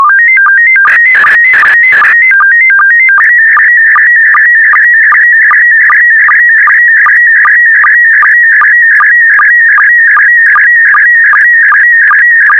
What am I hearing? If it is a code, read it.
SSTV (slow-scan television), PD 50.
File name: sstv / PD50-VK4MSL